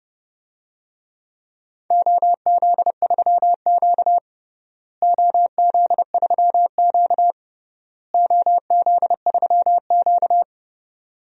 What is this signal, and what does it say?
Morse code, heard as a keyed tone, about 29 wpm, 700 Hz. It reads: OZ3Q OZ3Q OZ3Q
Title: DXCC-100 Call Signs - 4 Characters - Repeated 3x - 30wpm